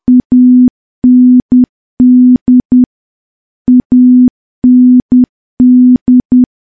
Morse code, heard as a keyed tone, about 10 wpm, 250 Hz. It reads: AND AND
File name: morse / morse-training-10WPM